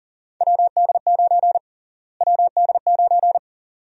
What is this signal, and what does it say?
Morse code, heard as a keyed tone, about 40 wpm, 700 Hz. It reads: WD9 WD9
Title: USA Call Sign Prefixes - Repeated 2x - 40wpm